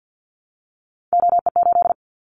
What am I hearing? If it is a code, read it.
Morse code, heard as a keyed tone, about 50 wpm, 700 Hz. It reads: OE8
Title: Callsign Prefixes for Top 25 DXCC - 50wpm